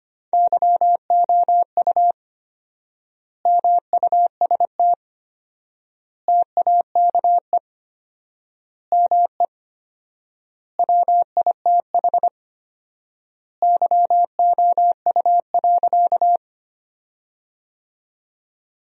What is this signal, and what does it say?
Morse code, heard as a keyed tone, about 25 wpm, 700 Hz. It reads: YOU MUST TAKE ME WITH YOU.
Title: 4x Word-Spacing - Sentences from Top 200 Words - 25wpm